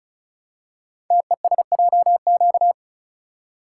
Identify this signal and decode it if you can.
Morse code, heard as a keyed tone, about 36 wpm, 700 Hz. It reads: TESJQ
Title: Code Groups - 5 Letters - 35wpm